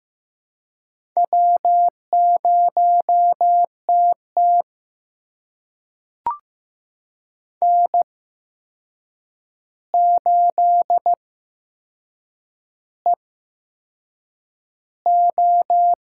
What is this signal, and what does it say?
Morse code, heard as a keyed tone, about 15 wpm, spaced as for 6 wpm, 700 Hz. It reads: W0TT N8EO
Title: USA Call Signs - 4 Characters - 8x Character Spacing - 15wpm